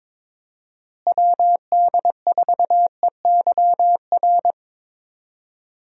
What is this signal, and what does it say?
Morse code, heard as a keyed tone, about 21 wpm, 700 Hz. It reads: WD4EYR
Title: USA Call Signs - 6 Characters - Repeated 3x - 22wpm